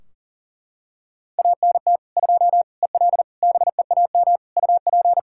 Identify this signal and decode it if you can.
Morse code, heard as a keyed tone, about 37 wpm, 700 Hz. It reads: ANT 2 EL BEAM UP
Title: Practice QSO - Exchange 2 - Info - 40wpm